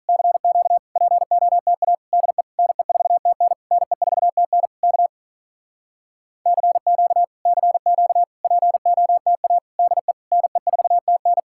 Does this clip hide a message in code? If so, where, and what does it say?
Yes: Morse code from 0.1 to 11.4 s, heard as a keyed tone, about 46 wpm, 700 Hz. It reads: CQ POTA DE NE4TN NE4TN K CQ CQ POTA DE NE4TN